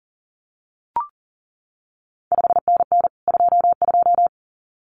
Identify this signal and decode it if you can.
Morse code, heard as a keyed tone, about 42 wpm, 700 Hz. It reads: 5NN 22
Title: CQ WW DX Contest Exchange Encore - 40wpm